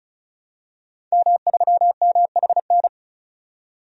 Morse code, heard as a keyed tone, about 34 wpm, 700 Hz. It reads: M3MHN
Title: DXCC-100 Call Signs - 5 Characters - 35wpm